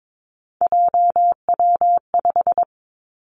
Morse code, heard as a keyed tone, about 22 wpm, 700 Hz. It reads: JW5
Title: Three Challenging Letter-Number Combinations 22wpm